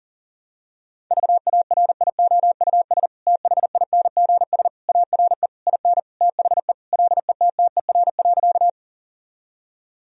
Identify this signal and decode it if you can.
Morse code, heard as a keyed tone, about 40 wpm, 700 Hz. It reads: VARIOUS THINGS ARE IN THE LETTER.
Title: Sentences from Top 1000 Words - Review Part 4 - 40wpm